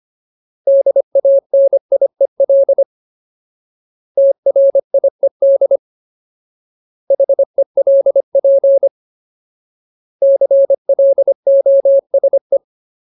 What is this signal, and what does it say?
Morse code, heard as a keyed tone, about 25 wpm, 550 Hz. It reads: DANIEL TRIED HELP CLOSE
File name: Ultimate Word List / 25WPM Ultimate